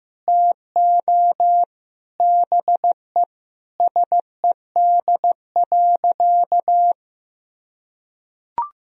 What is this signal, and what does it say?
Morse code, heard as a keyed tone, about 16 wpm, 700 Hz. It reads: TO BE SED.
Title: Sentences from Top 500 Words and Abbreviations 15wpm